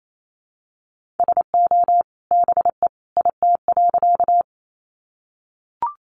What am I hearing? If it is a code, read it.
Morse code, heard as a keyed tone, about 28 wpm, 700 Hz. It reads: SO BE IT.
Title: Sentences from Top 100 Words 28wpm